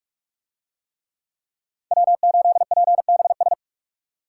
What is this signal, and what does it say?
Morse code, heard as a keyed tone, about 43 wpm, 700 Hz. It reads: W8PBS